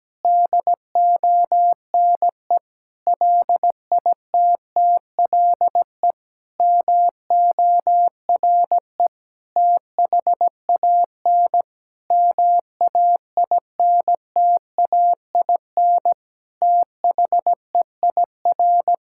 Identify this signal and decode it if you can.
Morse code, heard as a keyed tone, about 17 wpm, 700 Hz. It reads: DONE LITTLE MORE THAN MAINTAIN THEIR
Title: Sentences from Top 1000 Words - Review Part 5 - 17wpm